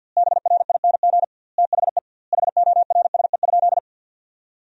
Morse code, heard as a keyed tone, about 46 wpm, 700 Hz. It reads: BRING THE HORSE?
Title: 2x Word-Spacing - Sentences from Top 500 Words - 50wpm